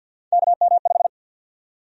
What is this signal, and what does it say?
Morse code, heard as a keyed tone, about 52 wpm, 700 Hz. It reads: KM5
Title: USA Call Sign Prefixes - Repeated 2x - 50wpm